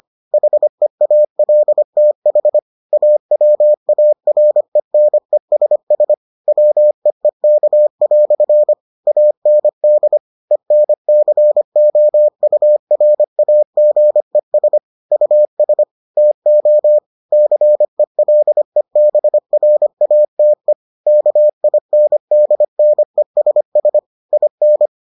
Morse code, heard as a keyed tone, about 25 wpm, 600 Hz. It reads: HEALTH AWARENESS WEEK" AND ENCOURAGES US TO CELEBRATE KINDNESS IN